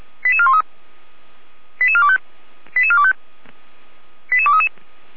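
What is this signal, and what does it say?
Five-tone selective call (ZVEI), heard as quick runs of five tones, tones of about 70 ms. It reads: 88413 88425 88425 88233
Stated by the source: ZVEI five-tone selective call received on SDR.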